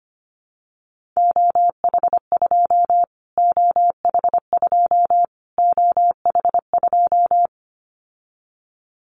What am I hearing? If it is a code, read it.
Morse code, heard as a keyed tone, about 25 wpm, 700 Hz. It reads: OH2 OH2 OH2
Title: Callsign Prefixes for Top 25 DXCC - Repeated 3x - 25wpm